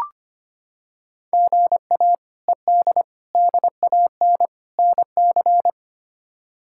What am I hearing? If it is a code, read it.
Morse code, heard as a keyed tone, about 25 wpm, 700 Hz. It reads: GA ED DAN NC
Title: K1USN SST QSO Practice - 25wpm